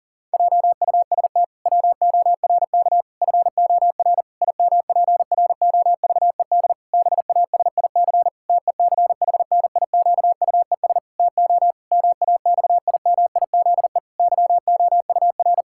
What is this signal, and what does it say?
Morse code, heard as a keyed tone, about 37 wpm, 700 Hz. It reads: JUST WORK FOR IMPROVED BASIC TECHNIQUES TO MAXIMIZE YOUR
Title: Pangram Challenge v2 - 40wpm